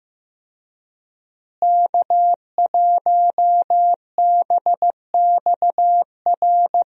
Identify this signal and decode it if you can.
Morse code, heard as a keyed tone, about 15 wpm, 700 Hz. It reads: K1BXR
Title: USA Call Signs - 5 Characters - Repeated 3x - 15wpm